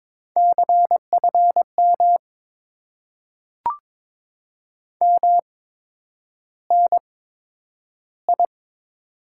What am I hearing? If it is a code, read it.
Morse code, heard as a keyed tone, about 22 wpm, spaced as for 9 wpm, 700 Hz. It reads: CFM MNI
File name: Top 90 QSO Elements - 8x Character Spacing - 22wpm